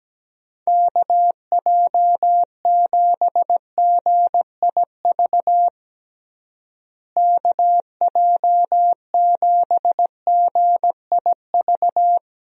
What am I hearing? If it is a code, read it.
Morse code, heard as a keyed tone, about 17 wpm, 700 Hz. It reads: KJ7GIV KJ7GIV
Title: USA Call Signs - 6 Characters - Repeated 2x - 17wpm